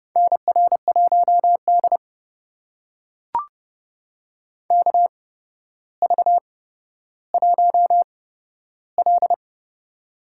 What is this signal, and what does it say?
Morse code, heard as a keyed tone, about 31 wpm, spaced as for 13 wpm, 700 Hz. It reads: NR1D KV1L
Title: USA Call Signs - 4 Characters - 8x Character Spacing - 30wpm